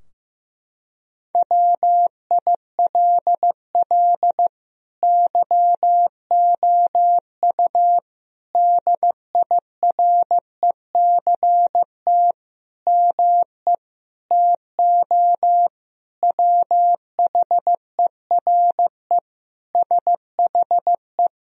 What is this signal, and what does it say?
Morse code, heard as a keyed tone, about 15 wpm, 700 Hz. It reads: WILL YOU DIRECT ME TO WHERE SHE
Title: Sentences from Top 1000 Words - Review Part 2 - 15wpm